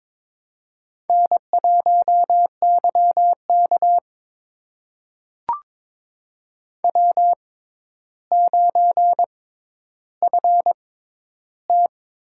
Morse code, heard as a keyed tone, about 22 wpm, spaced as for 11 wpm, 700 Hz. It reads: N1YK W9FT
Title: USA Call Signs - 4 Characters - 6x Character Spacing - 22wpm